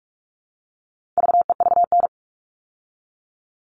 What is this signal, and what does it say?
Morse code, heard as a keyed tone, about 44 wpm, 700 Hz. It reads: VEVN
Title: All Cut Numbers - 4-Digit 45wpm